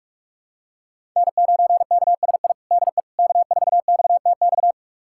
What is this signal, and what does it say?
Morse code, heard as a keyed tone, about 46 wpm, 700 Hz. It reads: N9KSI DE K4XTX
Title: USA Callsign DE Callsign - 5 Characters - 45wpm